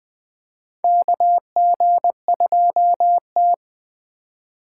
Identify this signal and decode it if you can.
Morse code, heard as a keyed tone, about 20 wpm, 700 Hz. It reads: KG2T
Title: USA Call Signs - 4 Characters - 4x Character Spacing - 20wpm